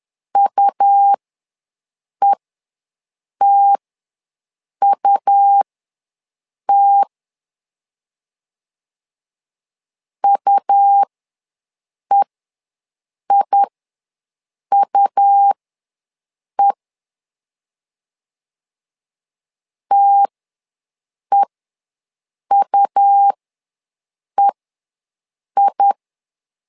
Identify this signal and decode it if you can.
Morse code, heard as a keyed tone, about 11 wpm, spaced as for 6 wpm, 800 Hz. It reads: UETUT UEIUE TEUEI
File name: LEZIONI / 2 UHDN / 2 U ws 10wpm 2017